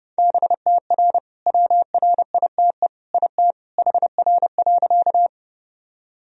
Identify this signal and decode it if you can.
Morse code, heard as a keyed tone, about 30 wpm, 700 Hz. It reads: BTR WRITE IT HR.
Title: Sentences from Top 400 Words and Abbreviations 30wpm